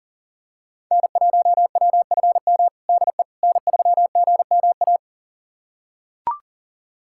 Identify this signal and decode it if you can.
Morse code, heard as a keyed tone, about 37 wpm, 700 Hz. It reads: N1WFM DE N3GMA
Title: USA Callsign DE Callsign - 5 Characters - 40wpm